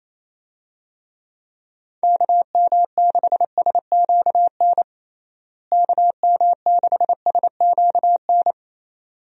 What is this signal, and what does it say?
Morse code, heard as a keyed tone, about 27 wpm, 700 Hz. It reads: KM6SQN KM6SQN
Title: USA Call Signs - 6 Characters - Repeated 2x - 28wpm